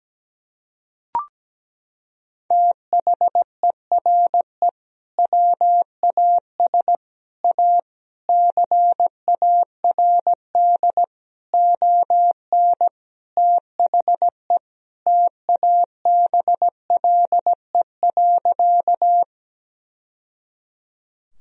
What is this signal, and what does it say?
Morse code, heard as a keyed tone, about 17 wpm, 700 Hz. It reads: THERE WAS A CARD ON THE TABLE.
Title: Sentences from Top 1000 Words - Review Part 4 - 17wpm